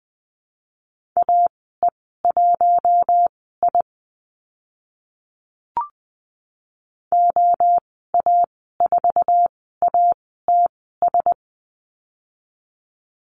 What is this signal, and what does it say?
Morse code, heard as a keyed tone, about 20 wpm, spaced as for 14 wpm, 700 Hz. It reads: AE1I OA4ATS
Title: 082 Lesson 14g - Number 2 - Call Signs 20wpm